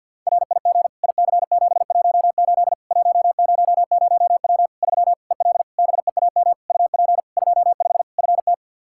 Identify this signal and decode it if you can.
Morse code, heard as a keyed tone, about 53 wpm, 700 Hz. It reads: RIG IC718 100W 3 EL BEAM UP 25 FT
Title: Practice QSO - Exchange 2 - Info - 50wpm